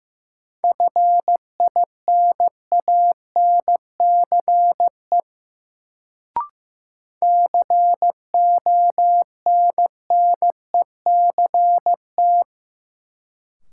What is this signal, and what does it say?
Morse code, heard as a keyed tone, about 15 wpm, 700 Hz. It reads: FINANCE CONNECT
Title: Sentences from Top 1500 Words - New Word Drill Alt - 15wpm